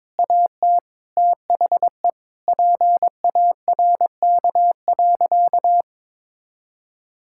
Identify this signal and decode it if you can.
Morse code, heard as a keyed tone, about 22 wpm, 690 Hz. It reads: AT THE PARK.
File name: practice / 22wpm / Easy Sentences with Hints - 22wpm